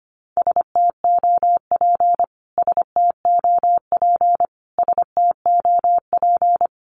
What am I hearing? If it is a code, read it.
Morse code, heard as a keyed tone, about 25 wpm, 700 Hz. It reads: STOP STOP STOP
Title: Top 500 Words Repeated 25wpm